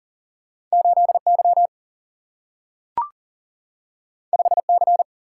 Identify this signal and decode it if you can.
Morse code, heard as a keyed tone, about 42 wpm, 700 Hz. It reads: 8Y 5C
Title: Single-Letter Single-Number 40wpm